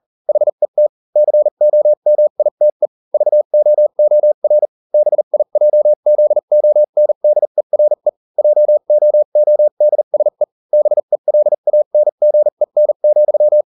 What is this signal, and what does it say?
Morse code, heard as a keyed tone, about 40 wpm, spaced as for 35 wpm, 600 Hz. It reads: HET COMITE VOOR BIJZONDERE JOODSE BELANGEN,